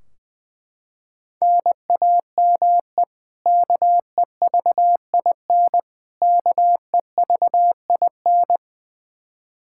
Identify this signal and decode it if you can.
Morse code, heard as a keyed tone, about 21 wpm, 700 Hz. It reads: NAME KEVIN KEVIN
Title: Practice QSO - Exchange 1 - Beginning - 20wpm